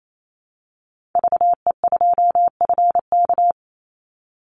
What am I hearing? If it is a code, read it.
Morse code, heard as a keyed tone, about 28 wpm, 700 Hz. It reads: VE2FK
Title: Call Signs - 28wpm